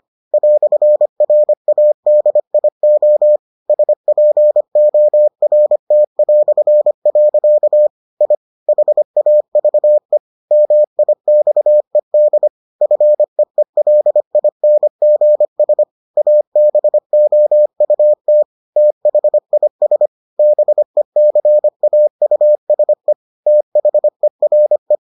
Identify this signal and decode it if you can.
Morse code, heard as a keyed tone, about 25 wpm, 600 Hz. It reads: "RADIO SPORT". I HAVE MIXED FEELINGS ABOUT THIS BECAUSE THERE